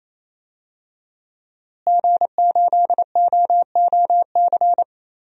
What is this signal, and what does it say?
Morse code, heard as a keyed tone, about 28 wpm, 700 Hz. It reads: G8OOC
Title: DXCC-100 Call Signs - 5 Characters - Repeated 3x - 28wpm